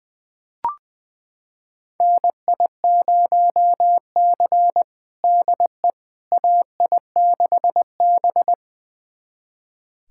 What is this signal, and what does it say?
Morse code, heard as a keyed tone, about 21 wpm, 700 Hz. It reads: NI0C DE AI6B
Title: USA Callsign DE Callsign - 4 Characters - 20wpm